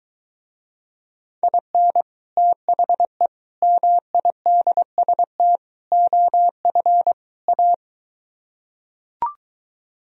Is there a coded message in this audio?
Yes: Morse code from 1.4 to 7.7 s, heard as a keyed tone, about 23 wpm, 700 Hz. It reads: IN THE MIDST OF A